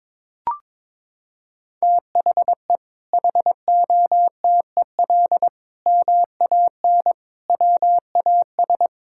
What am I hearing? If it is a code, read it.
Morse code, heard as a keyed tone, about 22 wpm, 700 Hz. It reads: THE HOTEL MAN WAS
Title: Sentences from Top 1600 Words 22wpm